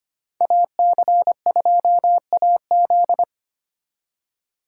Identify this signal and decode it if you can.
Morse code, heard as a keyed tone, about 30 wpm, 700 Hz. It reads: AC2AZ
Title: Speed Racing - USA 2x2 Call Signs 25wpm